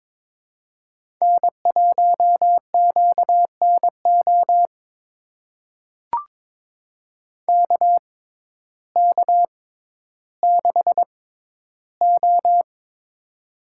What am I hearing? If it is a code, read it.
Morse code, heard as a keyed tone, about 22 wpm, spaced as for 12 wpm, 700 Hz. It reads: N1QNO KK6O
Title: USA Call Signs - 6x Character Spacing - 22wpm